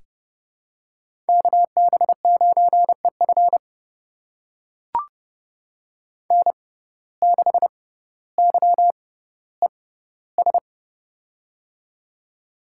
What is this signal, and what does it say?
Morse code, heard as a keyed tone, about 32 wpm, spaced as for 16 wpm, 700 Hz. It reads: KB9EF N6YES
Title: USA Call Signs - 5 Characters - 6x Character Spacing - 30wpm